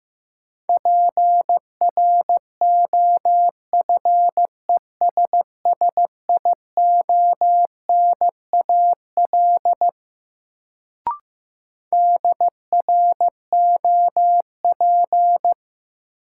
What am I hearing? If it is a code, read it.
Morse code, heard as a keyed tone, about 15 wpm, 700 Hz. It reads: PROFESSIONAL DROP
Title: Sentences from Top 1400 Words - New Word Drill Alt - 15wpm